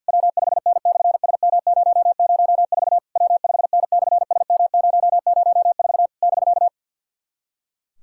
Morse code, heard as a keyed tone, about 50 wpm, 700 Hz. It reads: W5N/SM004 W5N/SM004 <BK>
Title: SOTA QSO Practice 50wpm